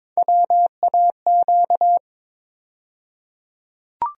Morse code, heard as a keyed tone, about 21 wpm, 700 Hz. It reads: WAQ
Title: USA Call Sign Postfixes - 3 Letters 22wpm